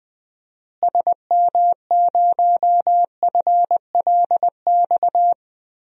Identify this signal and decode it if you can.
Morse code, heard as a keyed tone, about 21 wpm, 700 Hz. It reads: SM0FLX
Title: DXCC-100 Call Signs - 6 Characters - Repeated 3x - 20wpm